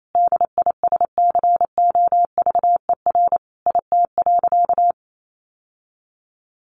Morse code, heard as a keyed tone, about 28 wpm, 700 Hz. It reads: DISCOVER IT.
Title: Sentences from Top 1100 Words 28wpm